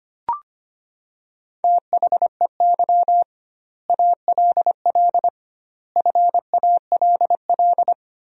Morse code, heard as a keyed tone, about 25 wpm, 700 Hz. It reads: THEY ALL FALL
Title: 2x Word-Spacing - Sentences from Top 500 Words - 25wpm